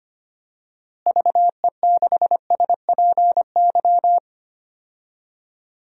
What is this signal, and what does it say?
Morse code, heard as a keyed tone, about 25 wpm, 700 Hz. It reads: VE6SPY
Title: DXCC-100 Call Signs - 6 Characters - Repeated 2x - 25wpm